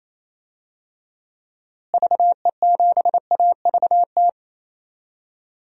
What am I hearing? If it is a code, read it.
Morse code, heard as a keyed tone, about 28 wpm, 700 Hz. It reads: VE7AVT
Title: DXCC-100 Call Signs - 6 Characters - Repeated 3x - 28wpm